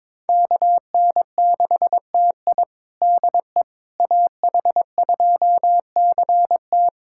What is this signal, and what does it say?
Morse code, heard as a keyed tone, about 22 wpm, 700 Hz. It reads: KN6TI DE AH2CT
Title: USA Callsign DE Callsign - 5 Characters - 22wpm